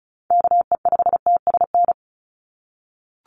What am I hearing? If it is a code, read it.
Morse code, heard as a keyed tone, about 34 wpm, 700 Hz. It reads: KE5TSN
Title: USA Call Signs 35wpm